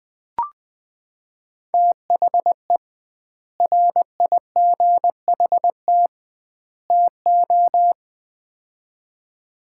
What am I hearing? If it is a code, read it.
Morse code, heard as a keyed tone, about 21 wpm, 700 Hz. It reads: THE RIGHT TO
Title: 2x Word-Spacing - Sets of 3 Words - 20wpm